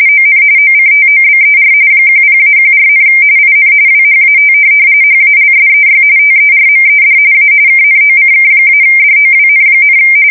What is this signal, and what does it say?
RTTY, about 45.45 baud. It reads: ON OCTOBER 10-14, IN WHAT HALSEY TERMED A KNOCK-DOWN,
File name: fsk170 / USS-New-Jersey